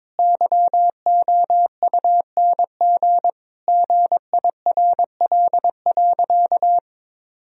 Morse code, heard as a keyed tone, about 22 wpm, 700 Hz. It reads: YOUNG GIRL.
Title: Sentences from Top 1000 Words - Review Part 4 - 22wpm